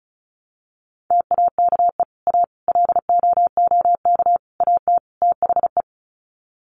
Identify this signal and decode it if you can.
Morse code, heard as a keyed tone, about 34 wpm, 700 Hz. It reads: TAKE A LOOK AT THE
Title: Sets of 5 Words 35wpm